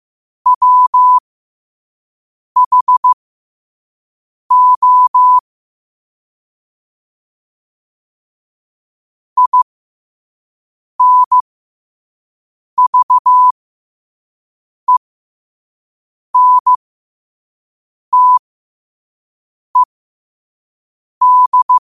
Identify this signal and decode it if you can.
Morse code, heard as a keyed tone, about 15 wpm, spaced as for 5 wpm, 1000 Hz. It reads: WHO INVENTED
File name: 5wpm